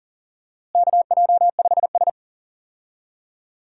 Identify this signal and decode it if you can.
Morse code, heard as a keyed tone, about 40 wpm, 700 Hz. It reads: KJ5S
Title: USA Call Signs - 4 Characters - Repeated 2x - 40wpm